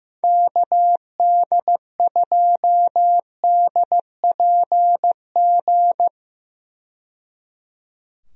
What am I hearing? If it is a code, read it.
Morse code, heard as a keyed tone, about 15 wpm, 700 Hz. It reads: KD2DPG
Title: USA Call Signs - 6 Characters - 15wpm